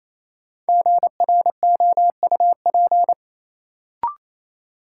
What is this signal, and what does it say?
Morse code, heard as a keyed tone, about 28 wpm, 700 Hz. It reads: GROUP
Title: Sentences from Top 1000 Words - Review Part 1 - New Word Drill - 28wpm